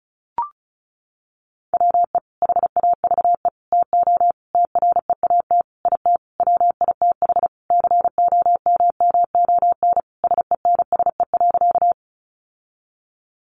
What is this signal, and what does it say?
Morse code, heard as a keyed tone, about 34 wpm, 700 Hz. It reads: WE HAVE TO TREAT IT WITH COMMON SENSE.
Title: Sentences from Top 1300 Words 35wpm